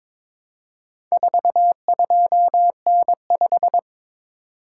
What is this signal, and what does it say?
Morse code, heard as a keyed tone, about 22 wpm, 700 Hz. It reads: 42N5
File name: practice / 22wpm / Four Letter-Number Combinations - 22wpm